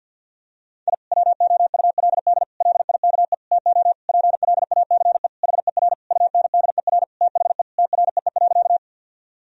Morse code, heard as a keyed tone, about 50 wpm, 700 Hz. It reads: I WOULD LIKE TO PLACE HER UNDER THE TREE.
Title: Sentences from Top 1000 Words - Review Part 1 - 50wpm